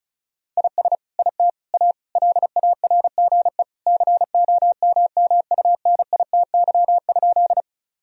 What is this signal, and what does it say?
Morse code, heard as a keyed tone, about 34 wpm, 700 Hz. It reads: IS IT A LARGE COMMUNITY?
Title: Sentences from Top 1000 Words - Review Part 4 - 35wpm